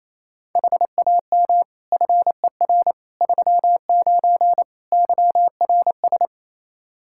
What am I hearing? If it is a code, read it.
Morse code, heard as a keyed tone, about 28 wpm, 700 Hz. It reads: HAM FER 39 YRS
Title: Practice QSO - Exchange 3 - More Info - 28wpm